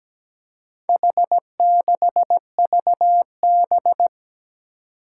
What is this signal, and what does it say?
Morse code, heard as a keyed tone, about 17 wpm, 700 Hz. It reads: H6VB
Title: Four Challenging Letter-Number Combinations 17wpm